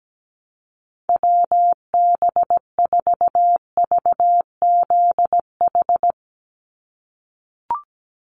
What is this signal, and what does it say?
Morse code, heard as a keyed tone, about 17 wpm, 700 Hz. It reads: WB4VZH
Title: USA Call Signs 17wpm